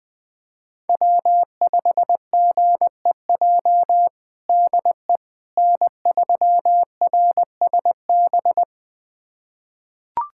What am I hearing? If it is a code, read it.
Morse code, heard as a keyed tone, about 21 wpm, 700 Hz. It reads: W5GEJ DE N3RSB
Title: USA Callsign DE Callsign - 5 Characters - 20wpm